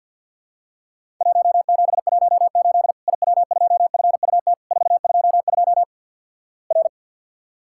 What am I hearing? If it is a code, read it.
Morse code, heard as a keyed tone, about 52 wpm, 700 Hz. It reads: 1718 IW2FUT 422 R
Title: Worked All Europe DX Contest - QTC Exchange with Repeats - 50wpm